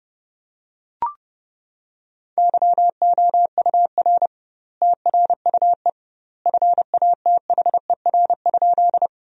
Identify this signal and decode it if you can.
Morse code, heard as a keyed tone, about 32 wpm, 700 Hz. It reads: YOUR TRUE FATHER?
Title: 2x Word-Spacing - Sentences from Top 400 Words - 30wpm